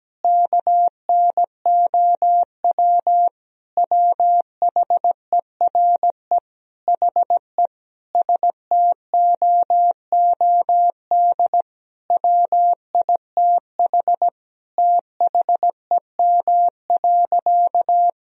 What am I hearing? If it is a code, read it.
Morse code, heard as a keyed tone, about 17 wpm, 700 Hz. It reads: KNOW WHERE HE STOOD WITH THEM.
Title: Sentences from Top 1000 Words - Review Part 2 - 17wpm